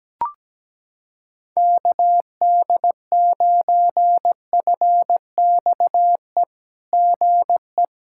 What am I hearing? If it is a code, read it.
Morse code, heard as a keyed tone, about 17 wpm, 700 Hz. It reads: KD9FXE GE
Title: SOTA QSO Practice 17wpm